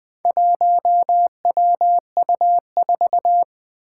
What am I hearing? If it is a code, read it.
Morse code, heard as a keyed tone, about 20 wpm, 700 Hz. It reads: 1WU4
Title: Four Challenging Letter-Number Combinations 20wpm